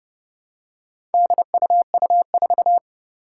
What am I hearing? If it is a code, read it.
Morse code, heard as a keyed tone, about 30 wpm, 700 Hz. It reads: DUU4
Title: Four Challenging Letter-Number Combinations 30wpm